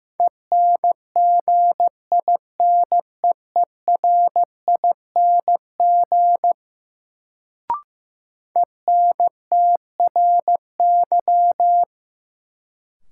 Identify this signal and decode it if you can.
Morse code, heard as a keyed tone, about 15 wpm, 700 Hz. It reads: ENGINEERING ENTRY
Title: Sentences from Top 1500 Words - New Word Drill - 15wpm